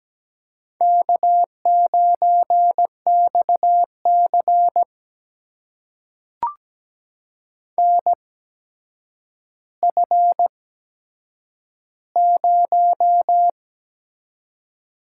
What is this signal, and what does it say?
Morse code, heard as a keyed tone, about 17 wpm, spaced as for 8 wpm, 700 Hz. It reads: K9XC NF0
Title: USA Call Signs - 4 Characters - 8x Character Spacing - 17wpm